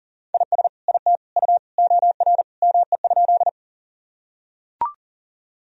Morse code, heard as a keyed tone, about 40 wpm, 700 Hz. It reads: IS IT U OR ME?
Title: Sentences from Top 200 Words and Abbreviations 40wpm